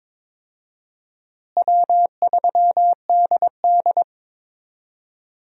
Morse code, heard as a keyed tone, about 22 wpm, 700 Hz. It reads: W3DD
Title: USA Call Signs - 4 Characters - Repeated 2x - 22wpm